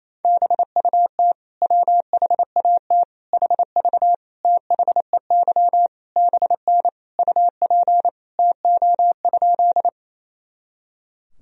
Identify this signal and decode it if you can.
Morse code, heard as a keyed tone, about 28 wpm, 700 Hz. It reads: BUT WHAT HV THEY BN UP TO?
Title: Sentences from Top 100 Words and Abbreviations 28wpm